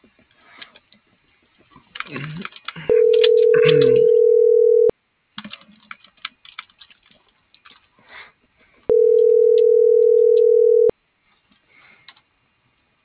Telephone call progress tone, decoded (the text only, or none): ringback tone